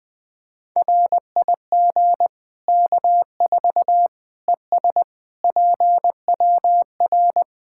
Morse code, heard as a keyed tone, about 21 wpm, 700 Hz. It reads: RIG K4 ES PWR
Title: K1USN SST QSO Practice 20wpm